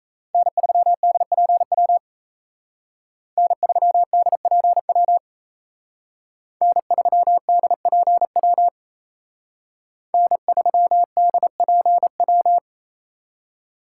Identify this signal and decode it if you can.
Morse code, heard as a keyed tone, about 32 wpm, 700 Hz. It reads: N3DPW N3DPW N3DPW N3DPW
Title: Speed Racing - USA 1x3 Call Signs 28wpm